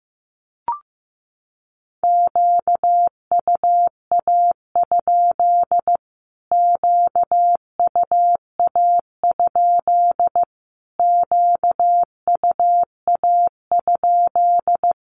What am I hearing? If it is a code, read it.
Morse code, heard as a keyed tone, about 15 wpm, 700 Hz. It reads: QUA? QUA? QUA?
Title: Q-Codes Repeated 3x 15wpm